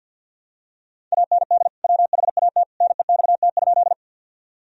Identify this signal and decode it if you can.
Morse code, heard as a keyed tone, about 48 wpm, 700 Hz. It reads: AND WHAT NEXT?
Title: Sentences from Top 1000 Words - Review Part 1 - 50wpm